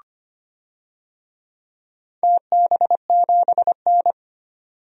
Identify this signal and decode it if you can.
Morse code, heard as a keyed tone, about 25 wpm, 700 Hz. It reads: TB7N
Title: DXCC-100 Call Signs - 4 Characters - 25wpm